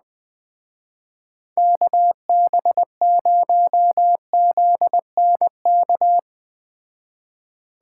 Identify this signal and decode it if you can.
Morse code, heard as a keyed tone, about 20 wpm, 700 Hz. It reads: KB0ZNK